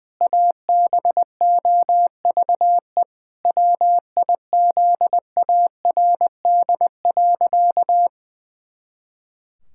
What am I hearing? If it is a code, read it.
Morse code, heard as a keyed tone, about 21 wpm, 700 Hz. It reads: ABOVE WIZARD.
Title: Pangram Challenge v2 - 20wpm